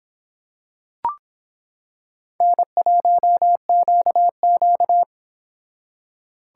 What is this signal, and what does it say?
Morse code, heard as a keyed tone, about 26 wpm, 700 Hz. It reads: N1QQ